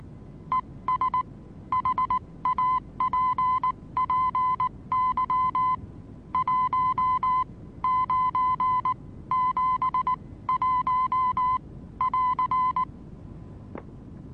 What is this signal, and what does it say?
Morse code, heard as a keyed tone, about 19 wpm, spaced as for 16 wpm, 1020 Hz. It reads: ES HAPPY 1 971 <AR>
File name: repos / Morsekurs / DJ4UF Morsen26